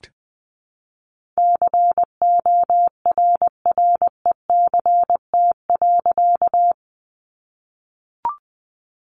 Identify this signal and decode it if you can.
Morse code, heard as a keyed tone, about 20 wpm, 700 Hz. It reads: CORRECT.
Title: A Sound Of Thunder 20wpm